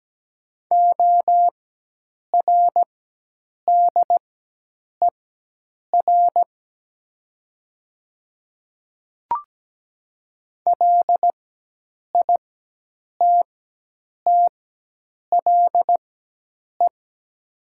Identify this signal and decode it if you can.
Morse code, heard as a keyed tone, about 17 wpm, spaced as for 8 wpm, 700 Hz. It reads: ORDER LITTLE
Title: 4x Character Spacing - Top 500 Words - 17wpm